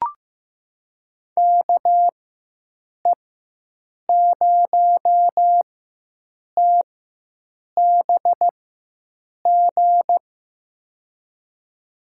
Morse code, heard as a keyed tone, about 15 wpm, spaced as for 7 wpm, 700 Hz. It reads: KE0TBG
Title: USA Call Signs - 6 Characters - 4x Character Spacing - 15wpm